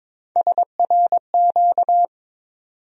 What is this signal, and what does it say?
Morse code, heard as a keyed tone, about 22 wpm, 700 Hz. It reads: SRQ
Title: USA Call Sign Postfixes - 3 Letters 22wpm